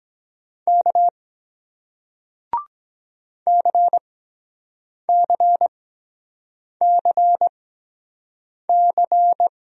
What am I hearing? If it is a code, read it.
Morse code, heard as a keyed tone, about 23 wpm, spaced as for 7 wpm, 700 Hz. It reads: K CCCC